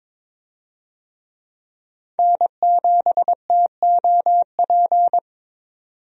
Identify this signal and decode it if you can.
Morse code, heard as a keyed tone, about 22 wpm, 700 Hz. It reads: N7TOP
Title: USA Call Signs - 5 Characters - 22wpm